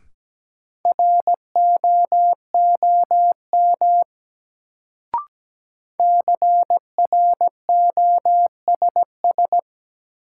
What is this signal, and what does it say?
Morse code, heard as a keyed tone, about 17 wpm, 700 Hz. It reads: ROOM CROSS
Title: Sentences from Top 300 Words - New Word Drill - 17wpm